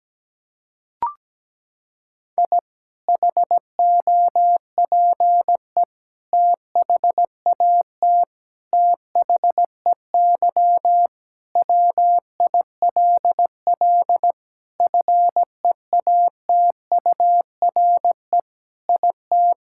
Morse code, heard as a keyed tone, about 17 wpm, 700 Hz. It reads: I HOPE THAT THEY WILL FEATURE IT
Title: Sentences from Top 1000 Words - Review Part 5 - 17wpm